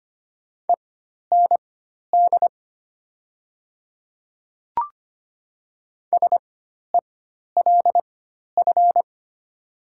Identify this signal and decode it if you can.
Morse code, heard as a keyed tone, about 25 wpm, spaced as for 12 wpm, 700 Hz. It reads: END SELF
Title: 4x Character Spacing - Top 200 Words - 25wpm